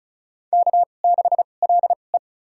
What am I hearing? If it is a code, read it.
Morse code, heard as a keyed tone, about 34 wpm, spaced as for 25 wpm, 700 Hz. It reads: K6LE
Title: USA Call Signs - 4 Characters - 2x Character Spacing - 35wpm